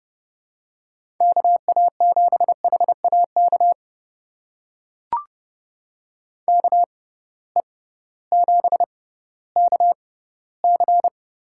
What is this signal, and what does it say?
Morse code, heard as a keyed tone, about 30 wpm, spaced as for 16 wpm, 700 Hz. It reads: KA7HAK KE7KC
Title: USA Call Signs - 6 Characters - 6x Character Spacing - 30wpm